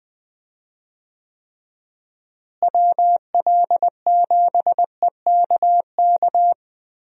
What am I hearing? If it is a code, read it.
Morse code, heard as a keyed tone, about 19 wpm, 700 Hz. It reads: WL7EKK
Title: USA Call Signs - 6 Characters - Repeated 3x - 20wpm